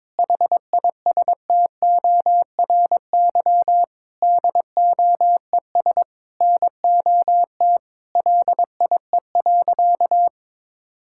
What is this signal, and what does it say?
Morse code, heard as a keyed tone, about 22 wpm, 700 Hz. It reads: HISTORY DOES NOT LIE.
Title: Sentences from Top 1600 Words 22wpm